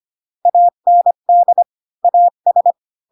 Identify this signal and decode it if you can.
Morse code, heard as a keyed tone, about 25 wpm, 700 Hz. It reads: AND AS
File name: Pinocchio / Pinocchio ch5 20 25 ik1hge